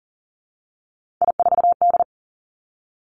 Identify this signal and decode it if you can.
Morse code, heard as a keyed tone, about 42 wpm, 700 Hz. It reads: I4D
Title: Three Challenging Letter-Number Combinations 40wpm